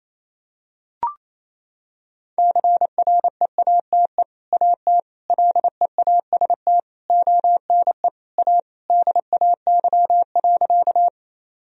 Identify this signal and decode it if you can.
Morse code, heard as a keyed tone, about 28 wpm, 700 Hz. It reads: CREATE AT LEAST ONE A DAY.
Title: Sentences from Top 1000 Words - Review Part 3 - 28wpm